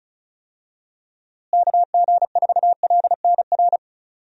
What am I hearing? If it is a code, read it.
Morse code, heard as a keyed tone, about 36 wpm, 700 Hz. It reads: KG4LNR
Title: USA Call Signs - 6 Characters - Repeated 3x - 35wpm